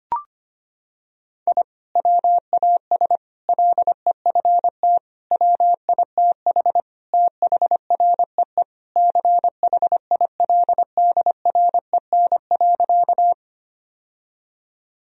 Morse code, heard as a keyed tone, about 25 wpm, 700 Hz. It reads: I WAS LEFT WITH THREE CHILDREN.
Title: Sentences from Top 1000 Words - Review Part 1 - 25wpm